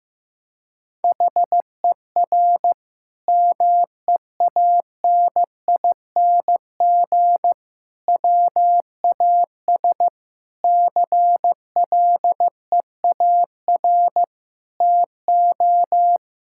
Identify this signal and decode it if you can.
Morse code, heard as a keyed tone, about 15 wpm, 700 Hz. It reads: HER MEANING WAS CLEAR TO
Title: Sentences from Top 1000 Words - Review Part 6 - 15wpm